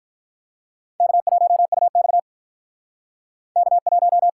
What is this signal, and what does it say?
Morse code, heard as a keyed tone, about 53 wpm, 700 Hz. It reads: K1UX K1